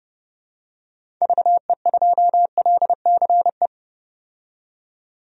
Morse code, heard as a keyed tone, about 32 wpm, 700 Hz. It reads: VE2LCE
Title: DXCC-100 Call Signs - 6 Characters - 30wpm